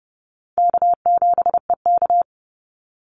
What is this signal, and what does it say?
Morse code, heard as a keyed tone, about 30 wpm, 700 Hz. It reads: K7EK
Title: Easy Call Signs Repeated 3x - 30wpm